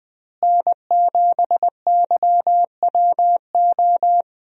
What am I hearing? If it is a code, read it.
Morse code, heard as a keyed tone, about 20 wpm, 700 Hz. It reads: N7YWO
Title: USA Call Signs - 5 Characters - Repeated 3x - 20wpm